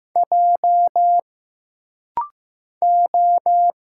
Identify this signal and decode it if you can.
Morse code, heard as a keyed tone, about 15 wpm, 700 Hz. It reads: J O
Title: Single Letters - Rapid-Fire 15wpm